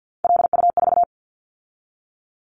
Morse code, heard as a keyed tone, about 50 wpm, 700 Hz. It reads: LU4
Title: Callsign Prefixes for Top 100 DXCC - 50wpm